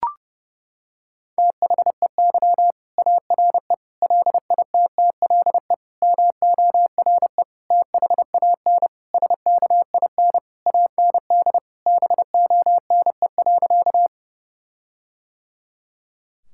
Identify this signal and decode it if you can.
Morse code, heard as a keyed tone, about 32 wpm, 700 Hz. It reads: THEY ARE LITTLE MORE THAN SKIN AND BONE.
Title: Sentences from Top 1600 Words 30wpm